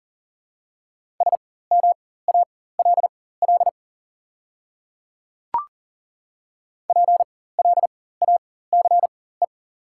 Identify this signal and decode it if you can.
Morse code, heard as a keyed tone, about 39 wpm, spaced as for 19 wpm, 690 Hz. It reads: SMALL PLACE
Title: 4x Character Spacing - Top 200 Words - 40wpm